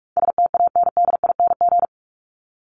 Morse code, heard as a keyed tone, about 44 wpm, 700 Hz. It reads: STANDING
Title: ING Suffix - Top 100 Words with Base Words 45wpm